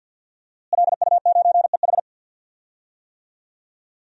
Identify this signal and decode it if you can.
Morse code, heard as a keyed tone, about 52 wpm, 700 Hz. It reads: LU9EH
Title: DXCC-100 Call Signs - 5 Characters - Repeated 3x - 50wpm